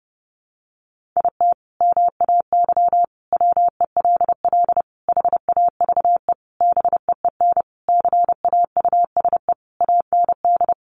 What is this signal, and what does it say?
Morse code, heard as a keyed tone, about 30 wpm, 700 Hz. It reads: IT MAY WELL HAVE BEEN CAUSE AND
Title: Sentences from Top 1000 Words 30wpm